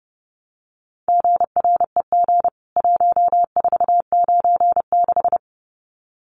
Morse code, heard as a keyed tone, about 30 wpm, 700 Hz. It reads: GREG 1496
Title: CWT Exchange 30wpm